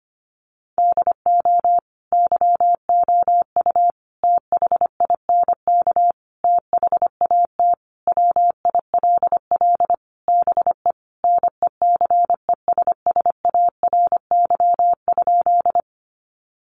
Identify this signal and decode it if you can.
Morse code, heard as a keyed tone, about 25 wpm, 700 Hz. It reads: DO YOU THINK THAT WILL BE NECESSARY?
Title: Sentences from Top 800 Words 25wpm